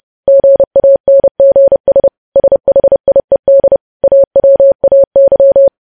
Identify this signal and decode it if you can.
Morse code, heard as a keyed tone, about 30 wpm, 550 Hz. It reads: GANGS SHIED AWAY
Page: news headlines in morse code at 30 wpm